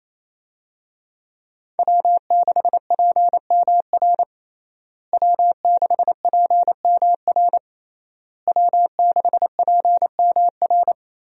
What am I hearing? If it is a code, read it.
Morse code, heard as a keyed tone, about 28 wpm, 700 Hz. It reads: W6PMR W6PMR W6PMR
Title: USA Call Signs - 5 Characters - Repeated 3x - 28wpm